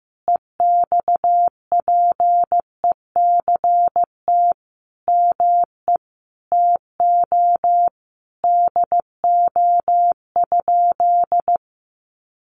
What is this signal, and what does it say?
Morse code, heard as a keyed tone, about 15 wpm, 700 Hz. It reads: EXPECT ME TO DO?
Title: Sentences from Top 1100 Words 15wpm